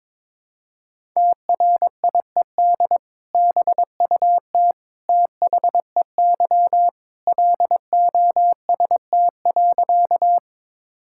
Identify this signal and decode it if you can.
Morse code, heard as a keyed tone, about 22 wpm, 700 Hz. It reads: TRIED BUT THEY LOST.
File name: practice / 22wpm / Sentences with 2-Word Phrases - 22wpm